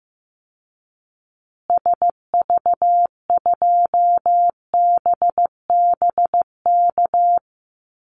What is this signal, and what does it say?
Morse code, heard as a keyed tone, about 15 wpm, 700 Hz. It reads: SV2BBK
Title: Call Signs Repeated 2x - 15wpm